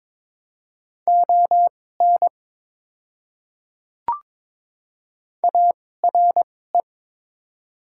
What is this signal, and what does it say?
Morse code, heard as a keyed tone, about 22 wpm, spaced as for 16 wpm, 700 Hz. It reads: ON ARE
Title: 2x Character Spacing - Top 100 Words - 22wpm